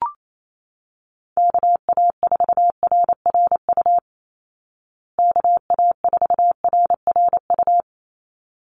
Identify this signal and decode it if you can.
Morse code, heard as a keyed tone, about 28 wpm, 700 Hz. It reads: KA4RRU KA4RRU
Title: Call Signs Repeated 3x - 28wpm